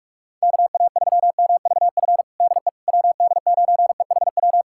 Morse code, heard as a keyed tone, about 46 wpm, 700 Hz. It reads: KA3MVF DE WD9EHW
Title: USA Callsign DE Callsign 45wpm